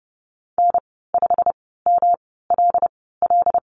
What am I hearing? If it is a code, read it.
Morse code, heard as a keyed tone, about 30 wpm, spaced as for 17 wpm, 700 Hz. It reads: N5MLL
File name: Lesson 19g - Letter w - Call Signs 30wpm